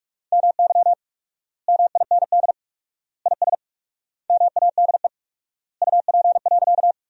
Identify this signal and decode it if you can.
Morse code, heard as a keyed tone, about 41 wpm, 700 Hz. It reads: MY MIND IS MADE UP.
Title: 4x Word-Spacing - Sentences from Top 500 Words - 45wpm